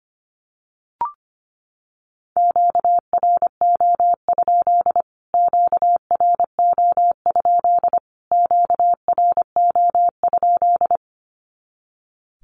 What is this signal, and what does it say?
Morse code, heard as a keyed tone, about 25 wpm, 700 Hz. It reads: QRO? QRO? QRO?
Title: Q-Codes Repeated 3x 25wpm